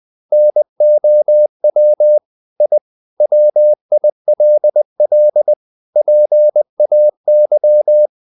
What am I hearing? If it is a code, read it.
Morse code, heard as a keyed tone, about 20 wpm, 600 Hz. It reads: NOW I WILL PAY
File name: the old man and the sea / the old man and the sea part 1 20wpm